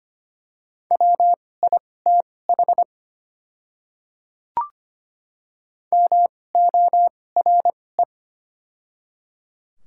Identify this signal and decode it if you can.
Morse code, heard as a keyed tone, about 25 wpm, spaced as for 18 wpm, 700 Hz. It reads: WITH MORE
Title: 2x Character Spacing - Top 100 Words - 25wpm